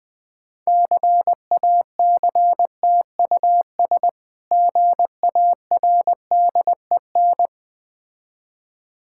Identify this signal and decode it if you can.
Morse code, heard as a keyed tone, about 21 wpm, 690 Hz. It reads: CACTUS GARDEN
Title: Difficult Words with Hints - 20wpm